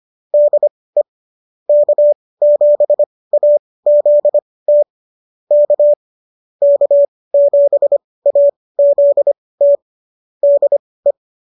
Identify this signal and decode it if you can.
Morse code, heard as a keyed tone, about 25 wpm, spaced as for 18 wpm, 590 Hz. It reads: DE K7AZT K K7AZT DE